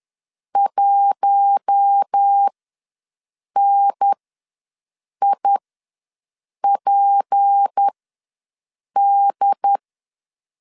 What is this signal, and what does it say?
Morse code, heard as a keyed tone, about 10 wpm, spaced as for 6 wpm, 800 Hz. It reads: 1NIPD
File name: LEZIONI / 7 GF9Z / 7 GF ws 10wpm 3918